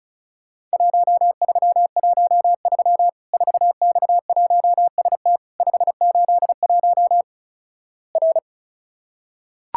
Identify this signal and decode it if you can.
Morse code, heard as a keyed tone, about 36 wpm, 700 Hz. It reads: 1313 4X1ST 581 R
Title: Worked All Europe DX Contest - QTC Exchange - 35wpm